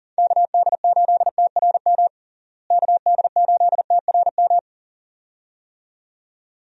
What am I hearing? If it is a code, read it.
Morse code, heard as a keyed tone, about 40 wpm, 700 Hz. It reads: KD8TRM KD8TRM
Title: USA Call Signs - 6 Characters - Repeated 2x - 40wpm